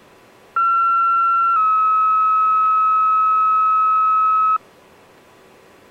Two-tone page, heard as a one-second tone, then a three-second tone, about 1360 then 1290 Hz. It is heard from 0.6 to 4.6 s.